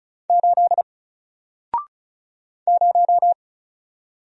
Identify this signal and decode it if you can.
Morse code, heard as a keyed tone, about 35 wpm, 700 Hz. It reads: 8 0
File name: Single-Letter-Number-35wpm